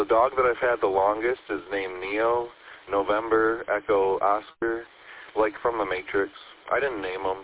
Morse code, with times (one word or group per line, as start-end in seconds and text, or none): none